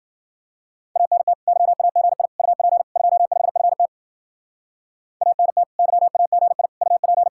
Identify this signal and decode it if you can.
Morse code, heard as a keyed tone, about 53 wpm, 700 Hz. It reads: ANT YAGI UP 25FT ANT YAGI UP